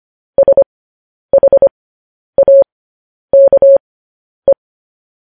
Morse code, heard as a keyed tone, about 25 wpm, spaced as for 10 wpm, 550 Hz. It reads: SHAKE
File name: podcast / News headlines-1x-25x10